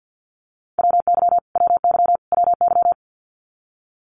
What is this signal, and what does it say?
Morse code, heard as a keyed tone, about 50 wpm, 700 Hz. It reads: WY WY WY
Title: US States Repeated 3x 50wpm